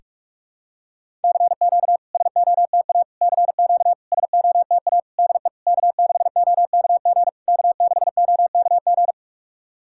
Morse code, heard as a keyed tone, about 43 wpm, 660 Hz. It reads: CQ SOTA CQ SOTA DE K6OKG K6OKG
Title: SOTA QSO Practice 45wpm